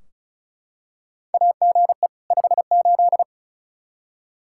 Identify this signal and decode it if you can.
Morse code, heard as a keyed tone, about 34 wpm, 700 Hz. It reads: AGE 58
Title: Practice QSO - Exchange 3 - More Info - 35wpm